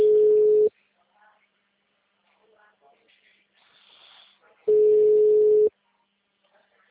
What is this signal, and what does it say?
Ringback tone (Europe, 425 Hz), with speech.
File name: out- -2016- - - .227